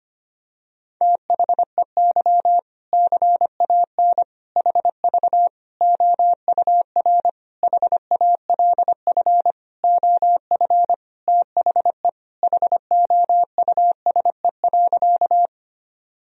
Morse code, heard as a keyed tone, about 25 wpm, 700 Hz. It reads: THEY CAN HV OUR HALF OF THE HOUSE.
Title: Sentences from Top 400 Words and Abbreviations 25wpm